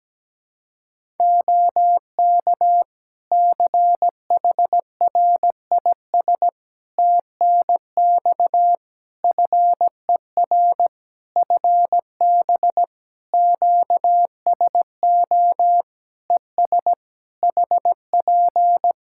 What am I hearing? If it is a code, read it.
Morse code, heard as a keyed tone, about 17 wpm, 700 Hz. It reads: OK CHRIS TNX FER FB QSO ES HP
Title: Practice QSO - Exchange 4 - Ending - 17wpm